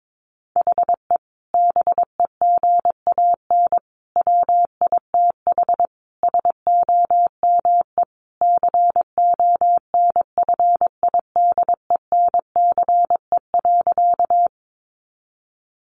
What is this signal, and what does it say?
Morse code, heard as a keyed tone, about 22 wpm, 700 Hz. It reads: HE BEGAN WITH SOME CONFIDENCE.
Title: Sentences from Top 1500 Words 22wpm